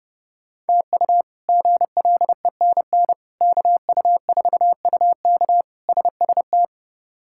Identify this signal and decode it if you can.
Morse code, heard as a keyed tone, about 30 wpm, 700 Hz. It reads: TU GLENN KU4UK SST
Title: K1USN SST QSO Practice - 30wpm